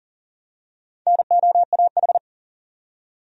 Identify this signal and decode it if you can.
Morse code, heard as a keyed tone, about 40 wpm, 700 Hz. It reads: NOAH
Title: USA Common Names - Max 4 Letters - 40wpm